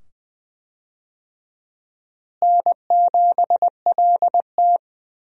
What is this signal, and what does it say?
Morse code, heard as a keyed tone, about 20 wpm, 700 Hz. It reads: N7LT
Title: USA Call Signs - 4 Characters - Repeated 2x - 20wpm